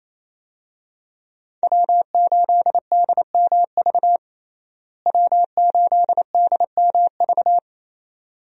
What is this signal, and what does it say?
Morse code, heard as a keyed tone, about 28 wpm, 700 Hz. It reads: W8DMV W8DMV
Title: USA Call Signs - 5 Characters - Repeated 2x - 28wpm